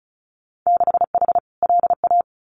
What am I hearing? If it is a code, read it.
Morse code, heard as a keyed tone, about 34 wpm, 700 Hz. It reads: 6H LA
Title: Winter Field Day Exchanges Encore 35wpm